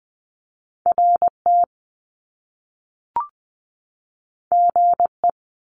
Morse code, heard as a keyed tone, about 20 wpm, 700 Hz. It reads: RT GE
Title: Top Two Letter Combinations in Words 20wpm